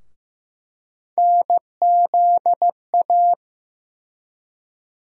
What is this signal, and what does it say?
Morse code, heard as a keyed tone, about 16 wpm, 700 Hz. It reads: NZA
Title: USA Call Sign Postfixes - 3 Letters 15wpm